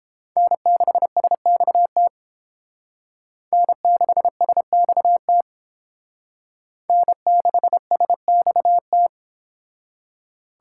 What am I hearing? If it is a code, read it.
Morse code, heard as a keyed tone, about 28 wpm, 700 Hz. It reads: N6SXT N6SXT N6SXT